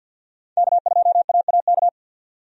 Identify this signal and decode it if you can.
Morse code, heard as a keyed tone, about 52 wpm, 700 Hz. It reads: K2AAK
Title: USA Call Signs - 5 Characters - 50wpm